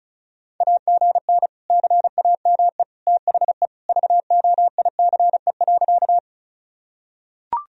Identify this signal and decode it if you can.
Morse code, heard as a keyed tone, about 34 wpm, 700 Hz. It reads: AGN CAME THE VOICE.
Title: Sentences from Top 500 Words and Abbreviations 35wpm